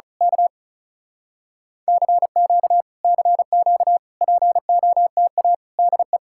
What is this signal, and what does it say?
Morse code, heard as a keyed tone, about 32 wpm, 700 Hz. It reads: K CQ CQ POTA DE
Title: Speed-Racing Common QSO Phrases Encore - 35wpm